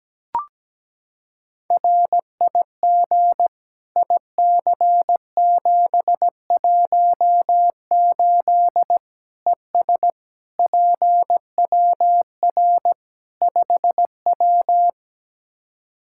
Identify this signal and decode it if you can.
Morse code, heard as a keyed tone, about 17 wpm, 700 Hz. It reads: RIG IC718 ES PWR 5W
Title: Practice QSO - Exchange 2 - Info - 17wpm